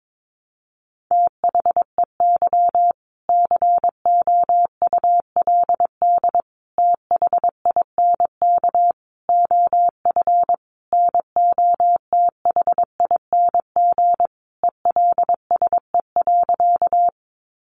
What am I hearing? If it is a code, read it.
Morse code, heard as a keyed tone, about 22 wpm, 700 Hz. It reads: THEY COULD THINK OF NOTHING ELSE.
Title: Sentences from Top 800 Words 22wpm